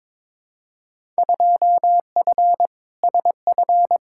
Morse code, heard as a keyed tone, about 22 wpm, 700 Hz. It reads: 2F SF
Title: Field Day Exchanges - 22wpm